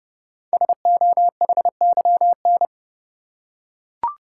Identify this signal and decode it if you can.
Morse code, heard as a keyed tone, about 30 wpm, 700 Hz. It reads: SOHYN
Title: Code Groups - 5 Letters - 30wpm